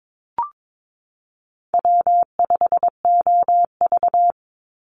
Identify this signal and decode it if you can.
Morse code, heard as a keyed tone, about 22 wpm, 700 Hz. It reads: W5OV
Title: Easy Call Signs - 22wpm